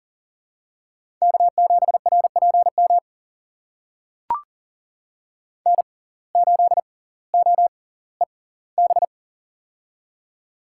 Morse code, heard as a keyed tone, about 39 wpm, spaced as for 20 wpm, 700 Hz. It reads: K7RPM N8OEB
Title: USA Call Signs - 5 Characters - 6x Character Spacing - 40wpm